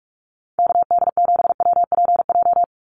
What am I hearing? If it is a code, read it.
Morse code, heard as a keyed tone, about 44 wpm, 700 Hz. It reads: KD7WPJ
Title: NA SOTA Activator Callsigns 45wpm